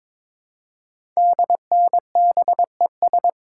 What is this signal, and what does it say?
Morse code, heard as a keyed tone, about 22 wpm, 700 Hz. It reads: DNBES
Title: Code Groups - 5 Letters - 22wpm